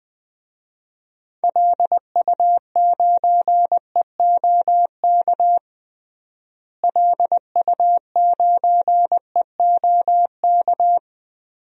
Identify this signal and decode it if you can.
Morse code, heard as a keyed tone, about 20 wpm, 700 Hz. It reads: LU9EOK LU9EOK
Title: DXCC-100 Call Signs - 6 Characters - Repeated 2x - 20wpm